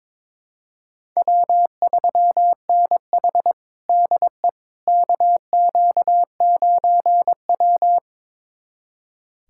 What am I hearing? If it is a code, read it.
Morse code, heard as a keyed tone, about 22 wpm, 700 Hz. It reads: W3NH DE KQ9W
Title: USA Callsign DE Callsign - 4 Characters - 22wpm